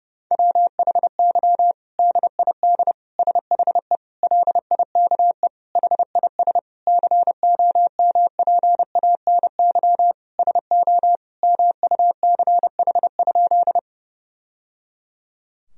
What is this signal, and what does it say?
Morse code, heard as a keyed tone, about 30 wpm, 700 Hz. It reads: WHY DID SHE LIKE HIS COMPANY SO MUCH?
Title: Sentences from Top 1000 Words - Review Part 3 - 30wpm